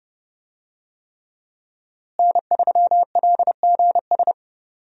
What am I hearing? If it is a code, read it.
Morse code, heard as a keyed tone, about 30 wpm, 700 Hz. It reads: N3LGS
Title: USA Call Signs - 5 Characters - Repeated 3x - 30wpm